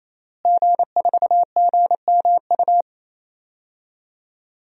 Morse code, heard as a keyed tone, about 28 wpm, 700 Hz. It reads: G4GMU
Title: DXCC-100 Call Signs - 5 Characters - Repeated 3x - 28wpm